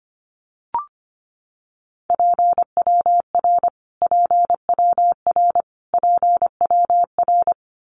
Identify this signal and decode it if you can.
Morse code, heard as a keyed tone, about 25 wpm, 700 Hz. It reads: PWR PWR PWR
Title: Abbreviations Repeated 3x - 25wpm